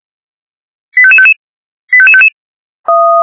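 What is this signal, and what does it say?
Five-tone selective call (ZVEI), heard as quick runs of five tones, tones of about 70 ms. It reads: 85555 85555
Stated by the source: ZVEI five-tone selective call 88914 received on an SDR.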